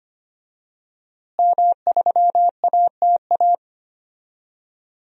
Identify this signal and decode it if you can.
Morse code, heard as a keyed tone, about 25 wpm, 700 Hz. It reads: M3ATA
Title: DXCC-100 Call Signs - 5 Characters - Repeated 3x - 25wpm